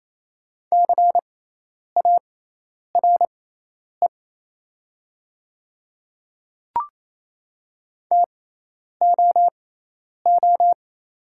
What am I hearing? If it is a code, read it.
Morse code, heard as a keyed tone, about 28 wpm, spaced as for 10 wpm, 700 Hz. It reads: CARE TOO
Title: 6x Character Spacing - Top 300 Words - 28wpm